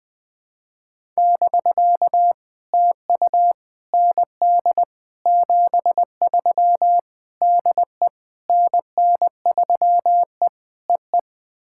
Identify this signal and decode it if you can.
Morse code, heard as a keyed tone, about 20 wpm, 700 Hz. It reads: <BK> TU ND 73 DE NN3E EE
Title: POTA QSO Practice - 20wpm